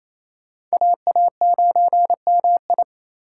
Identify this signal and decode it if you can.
Morse code, heard as a keyed tone, about 38 wpm, spaced as for 33 wpm, 700 Hz. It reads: AA9MI
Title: Speed Racing - USA 2x2 Call Signs 28wpm